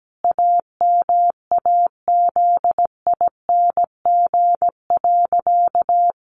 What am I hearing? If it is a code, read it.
Morse code, heard as a keyed tone, about 17 wpm, 700 Hz. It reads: AMAZING.
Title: Sentences from Top 1500 Words 17wpm